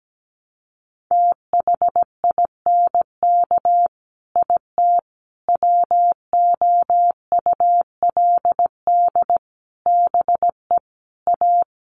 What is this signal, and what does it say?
Morse code, heard as a keyed tone, about 17 wpm, 700 Hz. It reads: THINK IT WOULD BE A
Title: Sets of 5 Words Encore2 17wpm